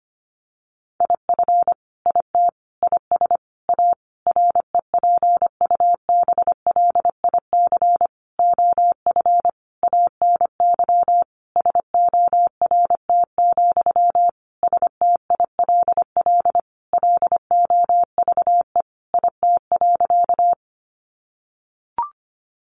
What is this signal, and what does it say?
Morse code, heard as a keyed tone, about 25 wpm, 700 Hz. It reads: IF IT IS A REPUBLIC OF ANY SORT, STILL LOVE IT.
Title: Sentences from Top 1500 Words 25wpm